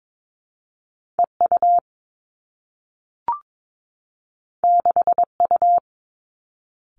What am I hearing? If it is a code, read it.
Morse code, heard as a keyed tone, about 22 wpm, 700 Hz. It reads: E U 6 U
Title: Two Challenging Letter-Number Combinations 22wpm